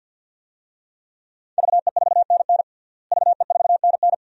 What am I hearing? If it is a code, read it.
Morse code, heard as a keyed tone, about 52 wpm, 700 Hz. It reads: VE4NN VE4NN
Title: DXCC-100 Call Signs - 5 Characters - Repeated 2x - 50wpm